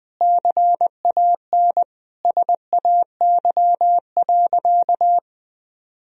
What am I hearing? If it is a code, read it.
Morse code, heard as a keyed tone, about 19 wpm, 700 Hz. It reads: CAN SAY.
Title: Sentences from Top 1000 Words - Review Part 2 - 20wpm